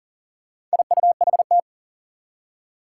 Morse code, heard as a keyed tone, about 39 wpm, 700 Hz. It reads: IUHT
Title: Four Challenging Letter-Number Combinations 40wpm